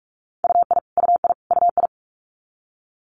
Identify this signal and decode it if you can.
Morse code, heard as a keyed tone, about 46 wpm, 690 Hz. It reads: UI UI UI
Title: Multi-letter Phonemes Repeated 3x 45wpm